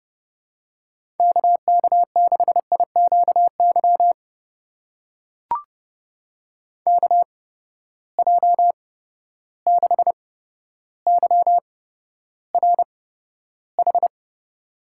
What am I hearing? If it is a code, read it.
Morse code, heard as a keyed tone, about 30 wpm, spaced as for 13 wpm, 700 Hz. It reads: KK6IQY KJ6YRH
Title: USA Call Signs - 6 Characters - 8x Character Spacing - 30wpm